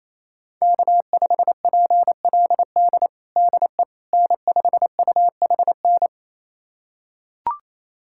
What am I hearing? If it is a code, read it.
Morse code, heard as a keyed tone, about 28 wpm, 700 Hz. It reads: K5PLD DE N5UHN
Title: USA Callsign DE Callsign - 5 Characters - 28wpm